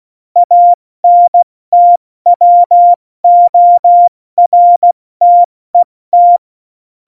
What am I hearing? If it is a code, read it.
Morse code, heard as a keyed tone, about 16 wpm, 700 Hz. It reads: ANTWORTET
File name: Morsekurs / Witze / Witz 02 Juni 2020, 70 F80